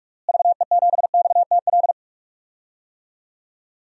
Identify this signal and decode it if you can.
Morse code, heard as a keyed tone, about 44 wpm, 700 Hz. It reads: VE7XTL
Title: DXCC-100 Call Signs - 6 Characters - Repeated 3x - 45wpm